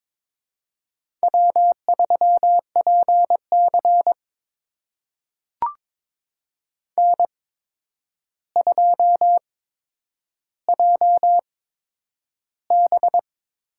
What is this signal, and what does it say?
Morse code, heard as a keyed tone, about 22 wpm, spaced as for 9 wpm, 700 Hz. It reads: W3PC N2JB
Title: USA Call Signs - 4 Characters - 8x Character Spacing - 22wpm